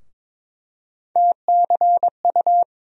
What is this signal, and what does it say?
Morse code, heard as a keyed tone, about 21 wpm, 700 Hz. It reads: TCU
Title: USA Call Sign Postfixes - 3 Letters 22wpm